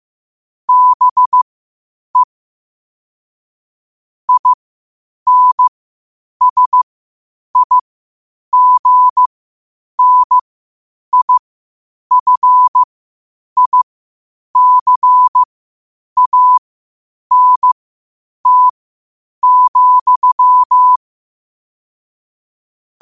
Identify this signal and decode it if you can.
Morse code, heard as a keyed tone, about 15 wpm, spaced as for 8 wpm, 1000 Hz. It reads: BE INSIGNIFICANT,
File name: qotdfiles / 8wpm